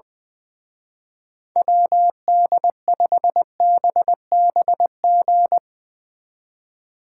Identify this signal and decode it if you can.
Morse code, heard as a keyed tone, about 21 wpm, 700 Hz. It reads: WD5BBG
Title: Speed Racing - USA 2x3 Call Signs 20wpm